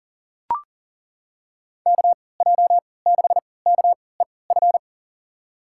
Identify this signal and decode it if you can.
Morse code, heard as a keyed tone, about 40 wpm, spaced as for 23 wpm, 700 Hz. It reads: KJ6KEF
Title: USA Call Signs - 6 Characters - 3x Character Spacing - 40wpm